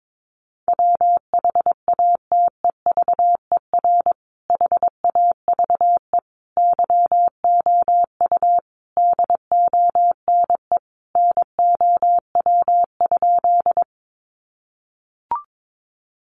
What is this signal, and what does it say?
Morse code, heard as a keyed tone, about 22 wpm, 700 Hz. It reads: WHATEVER HAVE YOU DONE NOW?
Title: Sentences from Top 1100 Words 22wpm